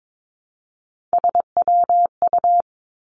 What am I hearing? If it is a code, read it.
Morse code, heard as a keyed tone, about 22 wpm, 700 Hz. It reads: SWU
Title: Three Challenging Letter-Number Combinations 22wpm